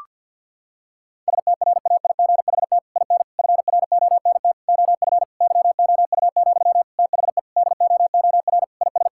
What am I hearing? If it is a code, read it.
Morse code, heard as a keyed tone, about 52 wpm, 700 Hz. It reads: STRAIGHT IN FRONT OF YOU, THE DOOR IS
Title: Sentences from Top 1000 Words - Review Part 6 - 50wpm